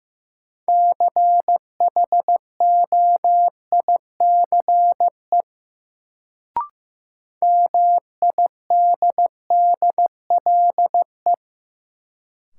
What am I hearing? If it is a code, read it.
Morse code, heard as a keyed tone, about 15 wpm, 700 Hz. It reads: CHOICE MIDDLE
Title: Sentences from Top 900 Words - New Word Drill - 15wpm